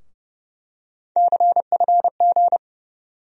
Morse code, heard as a keyed tone, about 30 wpm, 700 Hz. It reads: CFG
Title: USA Call Sign Postfixes - 3 Letters 30wpm